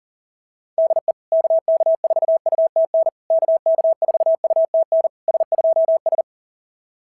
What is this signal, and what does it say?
Morse code, heard as a keyed tone, about 44 wpm, 690 Hz. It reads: DE KK4UTN KK4UTN S2S
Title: SOTA QSO Practice 40wpm